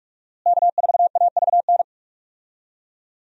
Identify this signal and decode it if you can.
Morse code, heard as a keyed tone, about 46 wpm, 700 Hz. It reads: K4AVN
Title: USA Call Signs - 5 Characters - Repeated 3x - 45wpm